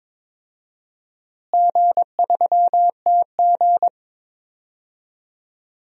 Morse code, heard as a keyed tone, about 21 wpm, 700 Hz. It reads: G3TG
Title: DXCC-100 Call Signs - 4 Characters - Repeated 2x - 22wpm